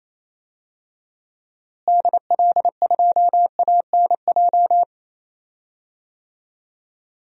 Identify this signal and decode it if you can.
Morse code, heard as a keyed tone, about 28 wpm, 700 Hz. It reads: DL2ANJ
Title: DXCC-100 Call Signs - 6 Characters - Repeated 3x - 28wpm